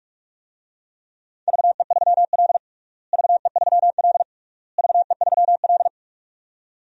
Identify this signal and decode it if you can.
Morse code, heard as a keyed tone, about 46 wpm, 700 Hz. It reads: VE3L VE3L VE3L
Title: DXCC-100 Call Signs - 4 Characters - Repeated 3x - 45wpm